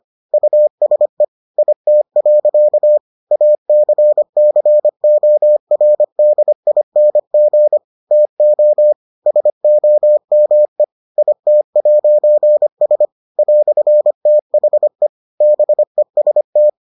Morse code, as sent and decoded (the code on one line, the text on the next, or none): ..- ... / .. - .-.-.- / .- -.-. -.-. --- .-. -.. .. -. --. / - --- / ... --- -- . / .. - .----. ... / .-..-. - .... / -... ... -
USE IT. ACCORDING TO SOME IT'S "THE BEST